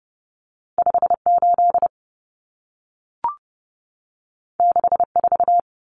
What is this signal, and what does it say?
Morse code, heard as a keyed tone, about 30 wpm, 700 Hz. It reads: 58 64
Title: Common Cut Numbers - 2-Digit 30wpm